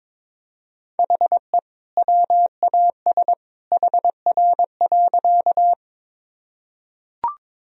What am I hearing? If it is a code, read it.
Morse code, heard as a keyed tone, about 22 wpm, 700 Hz. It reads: HE WAS HR.
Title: Sentences from Top 200 Words and Abbreviations 22wpm